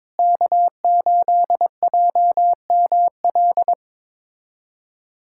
Morse code, heard as a keyed tone, about 22 wpm, 700 Hz. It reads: K8JML
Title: USA Callsign DE Callsign - 5 Characters - 22wpm